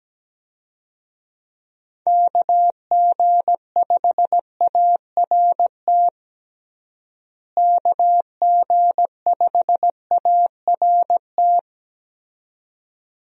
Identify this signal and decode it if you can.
Morse code, heard as a keyed tone, about 17 wpm, 700 Hz. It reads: KG5ART KG5ART
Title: USA Call Signs - 6 Characters - Repeated 2x - 17wpm